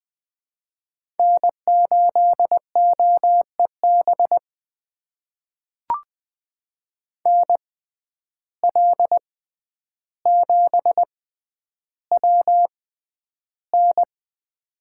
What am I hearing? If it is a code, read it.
Morse code, heard as a keyed tone, about 20 wpm, spaced as for 10 wpm, 700 Hz. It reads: N8OEB NL7WN
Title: USA Call Signs - 5 Characters - 6x Character Spacing - 20wpm